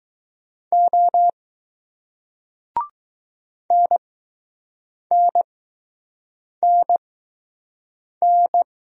Morse code, heard as a keyed tone, about 20 wpm, spaced as for 6 wpm, 700 Hz. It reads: O NNNN